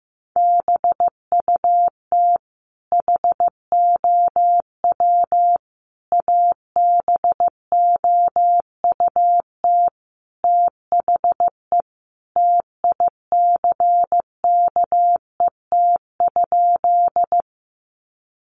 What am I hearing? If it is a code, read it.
Morse code, heard as a keyed tone, about 15 wpm, 700 Hz. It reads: BUT HOW ABOUT THE TICKET?
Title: Sentences from Top 1400 Words 15wpm